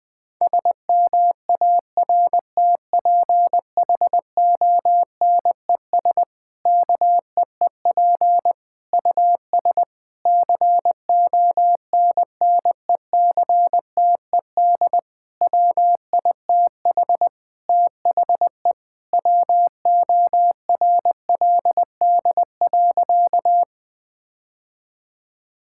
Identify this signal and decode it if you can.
Morse code, heard as a keyed tone, about 20 wpm, 700 Hz. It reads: SMARTPHONES KEEP US CONNECTED WITH THE WORLD.
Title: Difficult Sentences with Hints - 20wpm